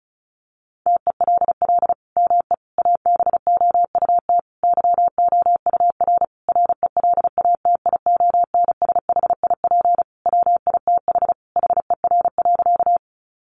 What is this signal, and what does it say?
Morse code, heard as a keyed tone, about 34 wpm, 700 Hz. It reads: TELL ME ABOUT YOUR RELATIONSHIP WITH HER.
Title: Sentences from Top 1100 Words 35wpm